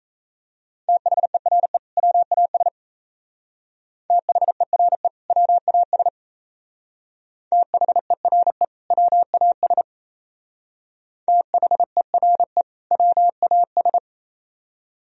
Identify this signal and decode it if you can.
Morse code, heard as a keyed tone, about 33 wpm, 700 Hz. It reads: THERE WAS THERE WAS THERE WAS THERE WAS
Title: Speed-Racing - Sets of 2 Words 28wpm